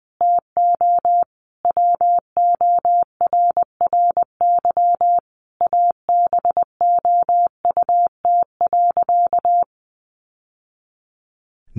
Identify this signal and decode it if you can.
Morse code, heard as a keyed tone, about 20 wpm, 700 Hz. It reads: TO WORRY ABOUT.
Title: Sentences from Top 1100 Words 20wpm